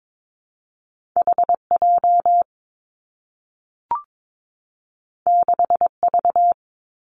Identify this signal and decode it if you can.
Morse code, heard as a keyed tone, about 22 wpm, 700 Hz. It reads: H J 6 V
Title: Two Challenging Letter-Number Combinations 22wpm